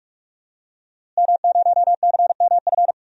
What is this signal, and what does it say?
Morse code, heard as a keyed tone, about 41 wpm, 700 Hz. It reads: M0CMF
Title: DXCC-100 Call Signs - 5 Characters - Repeated 2x - 45wpm